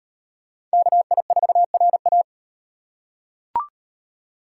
Morse code, heard as a keyed tone, about 38 wpm, 700 Hz. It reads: KI4RA